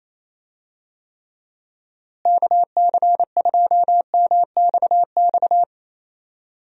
Morse code, heard as a keyed tone, about 27 wpm, 700 Hz. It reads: KC2MXX
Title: USA Call Signs - 6 Characters - Repeated 3x - 28wpm